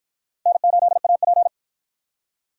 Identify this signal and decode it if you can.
Morse code, heard as a keyed tone, about 52 wpm, 700 Hz. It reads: N8AP